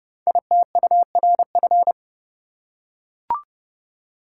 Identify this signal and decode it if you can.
Morse code, heard as a keyed tone, about 30 wpm, 700 Hz. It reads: ITURF
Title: Code Groups - 5 Letters - 30wpm